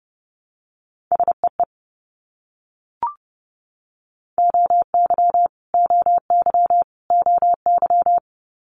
Morse code, heard as a keyed tone, about 30 wpm, 700 Hz. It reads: SEE OY OY OY
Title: Multi-letter Phonemes Repeated 3x 30wpm